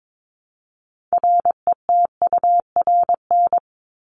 Morse code, heard as a keyed tone, about 22 wpm, 700 Hz. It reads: RETURN
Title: ED Suffix - Top 100 Words with Base Words 22wpm